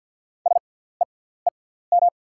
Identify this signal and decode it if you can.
Morse code, heard as a keyed tone, about 48 wpm, spaced as for 17 wpm, 700 Hz. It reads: SEEM
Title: 6x Character Spacing - Top 500 Words - 50wpm